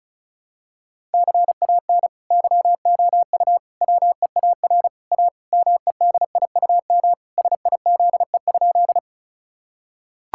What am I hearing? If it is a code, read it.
Morse code, heard as a keyed tone, about 34 wpm, 700 Hz. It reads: CAN YOU WEAR A MEDIUM SIZE?
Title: Sentences from Top 1000 Words - Review Part 6 - 35wpm